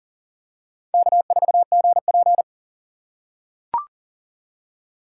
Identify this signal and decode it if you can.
Morse code, heard as a keyed tone, about 37 wpm, 700 Hz. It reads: K4GP
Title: USA Call Signs - 4 Characters - 6x Character Spacing - 40wpm